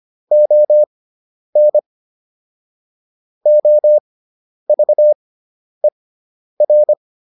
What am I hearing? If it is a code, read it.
Morse code, heard as a keyed tone, about 25 wpm, spaced as for 10 wpm, 600 Hz. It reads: ON OVER